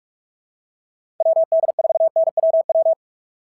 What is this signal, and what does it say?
Morse code, heard as a keyed tone, about 45 wpm, 660 Hz. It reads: WD4NWW
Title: SOTA QSO Practice 45wpm